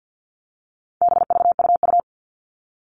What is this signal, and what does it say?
Morse code, heard as a keyed tone, about 50 wpm, 700 Hz. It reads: BVUU
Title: All Cut Numbers - 4-Digit 50wpm